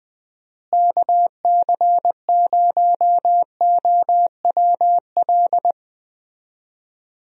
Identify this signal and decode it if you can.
Morse code, heard as a keyed tone, about 20 wpm, 700 Hz. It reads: KC0OWL
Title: USA Call Signs - 6 Characters - Repeated 3x - 20wpm